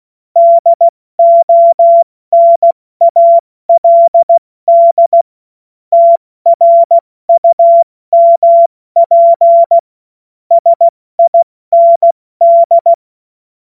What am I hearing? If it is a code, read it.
Morse code, heard as a keyed tone, about 16 wpm, 680 Hz. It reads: DONALD TRUMP SIND
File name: Morsekurs / Witze / Witz 09. Februar 2021, 70 F80